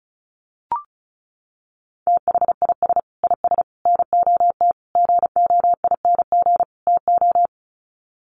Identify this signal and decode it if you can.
Morse code, heard as a keyed tone, about 34 wpm, 700 Hz. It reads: THIS IS NOT GOING TO
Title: Sets of 5 Words Encore2 35wpm